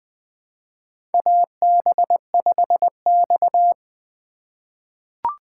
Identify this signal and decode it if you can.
Morse code, heard as a keyed tone, about 20 wpm, 700 Hz. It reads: AB5X
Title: USA Call Signs - 4 Characters - 3x Character Spacing - 20wpm